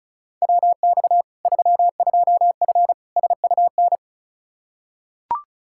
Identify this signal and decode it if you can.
Morse code, heard as a keyed tone, about 36 wpm, 700 Hz. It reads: WX 32F SUN
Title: Practice QSO - Exchange 2 - Info - 35wpm